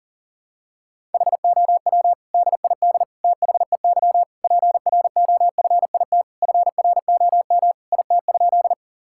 Morse code, heard as a keyed tone, about 40 wpm, 700 Hz. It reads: HOW DID THEY PROFIT FROM IT?
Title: Sentences from Top 1000 Words - Review Part 5 - 40wpm